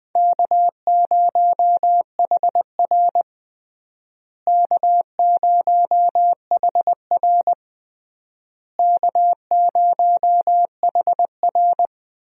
Morse code, heard as a keyed tone, about 20 wpm, 700 Hz. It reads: K0HR K0HR K0HR
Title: USA Call Signs - 4 Characters - Repeated 3x - 20wpm